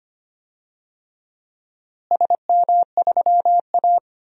Morse code, heard as a keyed tone, about 25 wpm, 700 Hz. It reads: SM3A
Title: DXCC-100 Call Signs - 4 Characters - Repeated 3x - 25wpm